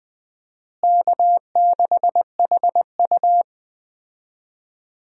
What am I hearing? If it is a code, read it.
Morse code, heard as a keyed tone, about 20 wpm, 700 Hz. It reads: K6HU
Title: USA Call Signs - 4 Characters - Repeated 3x - 20wpm